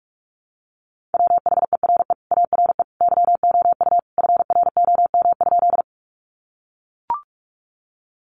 Morse code, heard as a keyed tone, about 44 wpm, 700 Hz. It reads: WHERE ARE YOU FROM?
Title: Sentences from Top 200 Words 45wpm